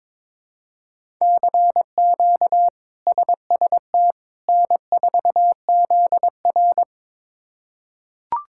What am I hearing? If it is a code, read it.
Morse code, heard as a keyed tone, about 22 wpm, 700 Hz. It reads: CQ SST N4ZR
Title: K1USN SST QSO Practice - 22wpm